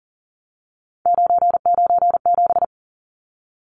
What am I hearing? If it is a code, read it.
Morse code, heard as a keyed tone, about 40 wpm, 700 Hz. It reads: 997
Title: Numbers - 3-Digit 40wpm